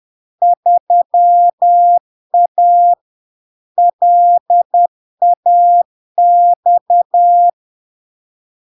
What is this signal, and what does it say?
Morse code, heard as a keyed tone, about 10 wpm, 700 Hz. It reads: 3A LAX
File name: LARK-FD-2022-Log-in-CW-10wpm-700-Hz